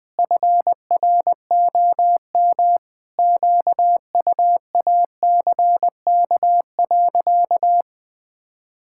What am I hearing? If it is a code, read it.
Morse code, heard as a keyed tone, about 20 wpm, 700 Hz. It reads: FROM QUACK.
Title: Pangram Challenge v2 - 20wpm